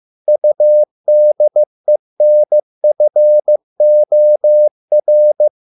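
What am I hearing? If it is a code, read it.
Morse code, heard as a keyed tone, about 15 wpm, 600 Hz. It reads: UDENFOR
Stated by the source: Morse code at a slow 15 wpm from CW med Gnister.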